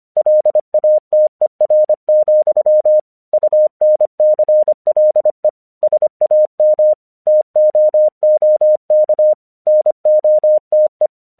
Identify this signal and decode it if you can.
Morse code, heard as a keyed tone, about 24 wpm, 610 Hz. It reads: LATER, UNCLE SAM TOOK NOTE
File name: 1 English 25wpm / AQC09